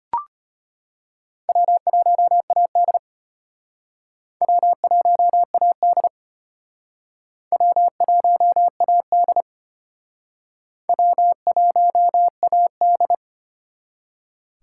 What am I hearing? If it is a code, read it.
Morse code, heard as a keyed tone, about 33 wpm, 700 Hz. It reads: W1AD W1AD W1AD W1AD